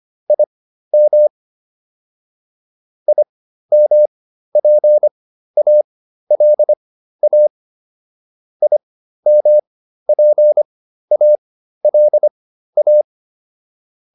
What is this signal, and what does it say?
Morse code, heard as a keyed tone, about 25 wpm, spaced as for 13 wpm, 600 Hz. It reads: IM IMPALA IMPALA